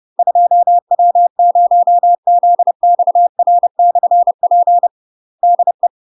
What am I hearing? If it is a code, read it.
Morse code, heard as a keyed tone, about 30 wpm, 700 Hz. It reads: 2W0ZXR/P DE
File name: UK MASTER QSOs TRAINING / 30WPM / UK MASTER QSOs 0038